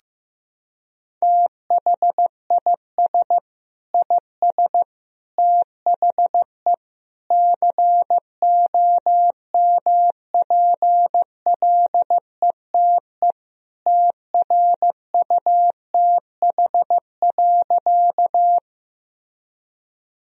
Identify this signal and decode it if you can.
Morse code, heard as a keyed tone, about 16 wpm, 700 Hz. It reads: THIS IS THE COMPLETE TRUTH.
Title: Sentences from Top 1000 Words - Review Part 2 - 15wpm